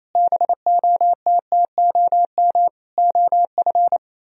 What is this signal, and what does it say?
Morse code, heard as a keyed tone, about 32 wpm, 700 Hz. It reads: BOTTOM OF
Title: Speed-Racing - Sets of 4 Words 28wpm